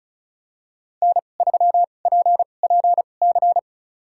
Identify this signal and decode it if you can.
Morse code, heard as a keyed tone, about 36 wpm, spaced as for 25 wpm, 700 Hz. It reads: N3PPC
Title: USA Call Signs - 5 Characters - 2x Character Spacing - 35wpm